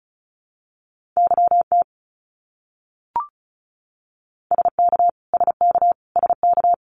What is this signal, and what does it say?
Morse code, heard as a keyed tone, about 34 wpm, 700 Hz. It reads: YT SK SK SK
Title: Canadian Provinces and Territories - Repeated 3x - 35wpm